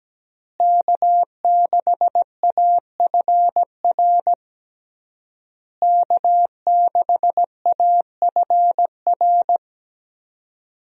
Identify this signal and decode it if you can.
Morse code, heard as a keyed tone, about 17 wpm, 700 Hz. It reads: K6AFR K6AFR
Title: USA Call Signs - 5 Characters - Repeated 3x - 17wpm